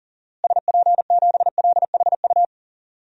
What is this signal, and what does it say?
Morse code, heard as a keyed tone, about 40 wpm, 700 Hz. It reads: SP7LHU
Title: DXCC-100 Call Signs - 6 Characters - Repeated 3x - 40wpm